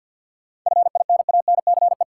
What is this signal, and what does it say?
Morse code, heard as a keyed tone, about 53 wpm, 700 Hz. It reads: FINANCE
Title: Sentences from Top 1500 Words - New Word Drill Alt - 50wpm